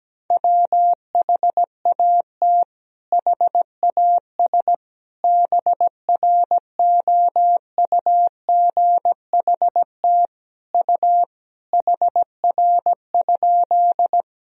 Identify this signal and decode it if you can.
Morse code, heard as a keyed tone, about 17 wpm, 700 Hz. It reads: WHAT HAS BROUGHT U HR?
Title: Sentences from Top 500 Words and Abbreviations 17wpm